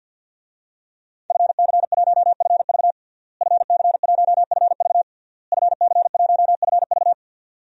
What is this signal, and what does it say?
Morse code, heard as a keyed tone, about 52 wpm, 700 Hz. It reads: FC1FV FC1FV FC1FV
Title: DXCC-100 Call Signs - 5 Characters - Repeated 3x - 50wpm